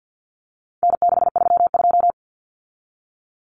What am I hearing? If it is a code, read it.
Morse code, heard as a keyed tone, about 50 wpm, 700 Hz. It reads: N632
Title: Common Cut Numbers - 4-Digit 50wpm